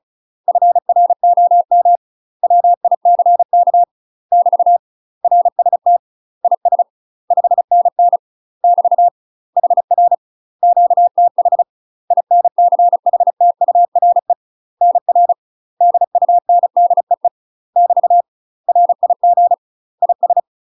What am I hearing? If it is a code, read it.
Morse code, heard as a keyed tone, about 33 wpm, 700 Hz. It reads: FROM WICK <BT> RST IS 5NN <BT> HR QTH INCHTURE NR DUNDEE <BT> RIG IS
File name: UK MASTER QSOs TRAINING / 35WPM / UK MASTER QSOs 0041